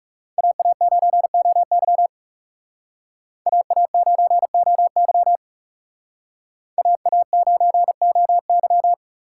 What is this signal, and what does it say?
Morse code, heard as a keyed tone, about 36 wpm, 700 Hz. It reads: AA9OY AA9OY AA9OY
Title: Speed Racing - USA 2x2 Call Signs 30wpm